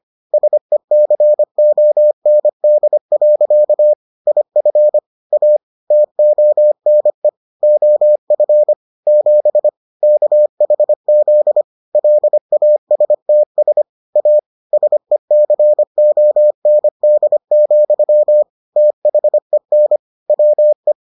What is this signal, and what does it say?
Morse code, heard as a keyed tone, about 25 wpm, 600 Hz. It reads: SECOND. IF A TONE OF 7 KHZ LASTS A SECOND, THEN WE